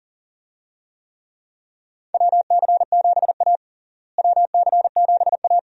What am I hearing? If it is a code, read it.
Morse code, heard as a keyed tone, about 40 wpm, 700 Hz. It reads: WC7A WC7A
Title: USA Call Signs - 4 Characters - Repeated 2x - 40wpm